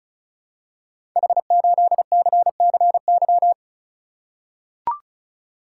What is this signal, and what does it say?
Morse code, heard as a keyed tone, about 34 wpm, 700 Hz. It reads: H8CCY
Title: Code Groups - 5 Letter:Numbers - 35wpm